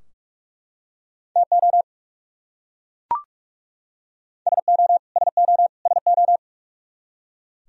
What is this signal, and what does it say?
Morse code, heard as a keyed tone, about 42 wpm, 700 Hz. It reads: TO SO SO SO
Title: Top 100 Words Repeated 45wpm